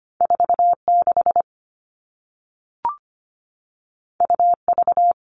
Morse code, heard as a keyed tone, about 25 wpm, 700 Hz. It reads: 46 UV
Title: All Cut Numbers - 2-Digit 25wpm